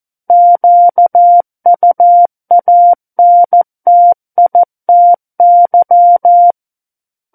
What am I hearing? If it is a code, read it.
Morse code, heard as a keyed tone, about 14 wpm, 700 Hz. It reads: QUANTITY
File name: Book One / 05-Chapter 5 - 14 WPM